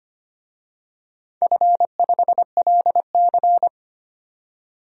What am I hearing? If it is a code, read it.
Morse code, heard as a keyed tone, about 25 wpm, 700 Hz. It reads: F5LC
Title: DXCC-100 Call Signs - 4 Characters - 25wpm